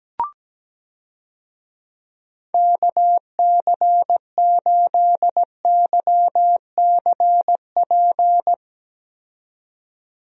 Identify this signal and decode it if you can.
Morse code, heard as a keyed tone, about 17 wpm, 700 Hz. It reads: KC8YCP
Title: USA Call Signs - 6 Characters - 17wpm